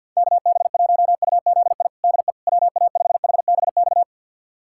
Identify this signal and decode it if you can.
Morse code, heard as a keyed tone, about 51 wpm, 700 Hz. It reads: KB1UZI DE WA5HBX
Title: USA Callsign DE Callsign 50wpm